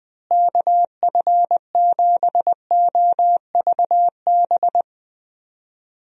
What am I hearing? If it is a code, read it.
Morse code, heard as a keyed tone, about 20 wpm, 700 Hz. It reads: KF7OVB
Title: USA Call Signs - 6 Characters - Repeated 2x - 20wpm